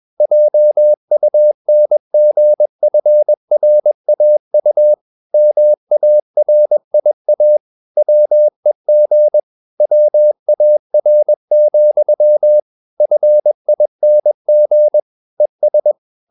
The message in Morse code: JUNGFRAU MARIA WEG WAR, FING ES

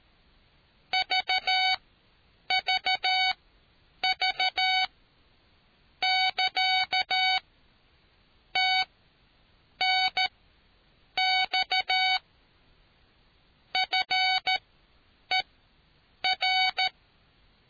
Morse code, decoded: VVV<KA> TNX FER

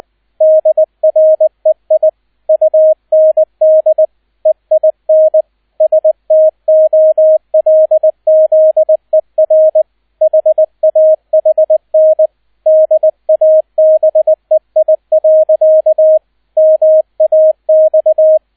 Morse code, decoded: DREI UND EIN STOLZER HAHN DABEI. MAX

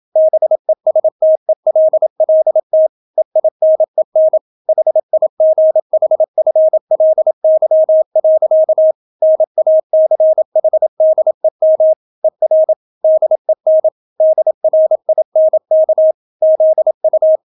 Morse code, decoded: BESTELLT EINEN HIGHFLY. NACHDEM ER DEN DRINK ZU

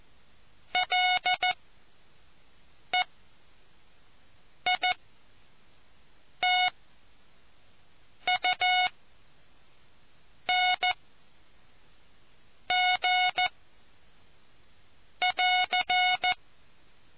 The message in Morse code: LEITUNG <AR>

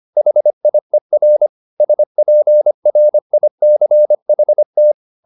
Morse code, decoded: HIER SPRICHT

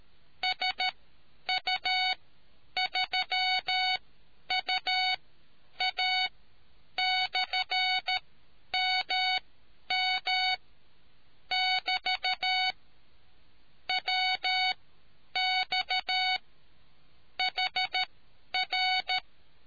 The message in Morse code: SU3UA/MM <BT> WX HR

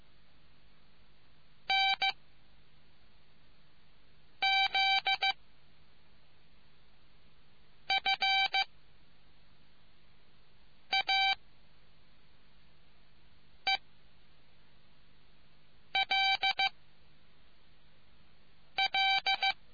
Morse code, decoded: N Z F A E L L

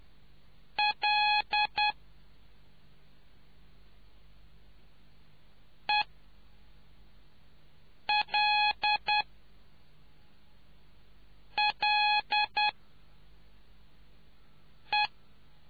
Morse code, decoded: L E L L E

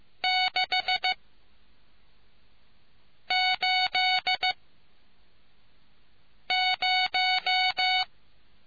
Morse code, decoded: 6 8 0

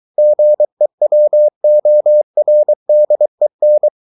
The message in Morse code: GEWORDEN

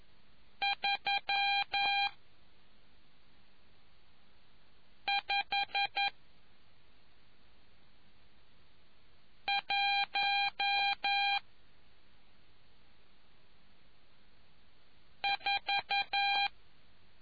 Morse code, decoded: 3 5 1 4

